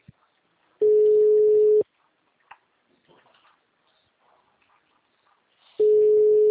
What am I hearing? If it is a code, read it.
Ringback tone (Europe, 425 Hz).
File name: out- -2005- - - .12462